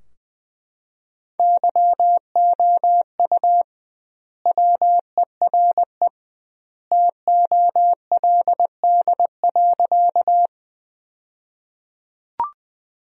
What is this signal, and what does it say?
Morse code, heard as a keyed tone, about 20 wpm, 700 Hz. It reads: YOU WERE TOLD.
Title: 2x Word-Spacing - Sentences from Top 400 Words - 20wpm